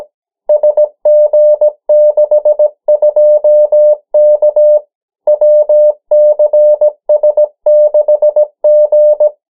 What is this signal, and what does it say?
Morse code, heard as a keyed tone, about 17 wpm, 600 Hz. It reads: SG62K WCS6G
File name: morse mp3 / 14 urok 13